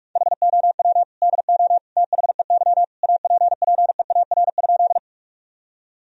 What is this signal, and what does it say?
Morse code, heard as a keyed tone, about 47 wpm, 700 Hz. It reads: HOW DO THEY APPEAR?
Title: Sentences from Top 1000 Words - Review Part 2 - 45wpm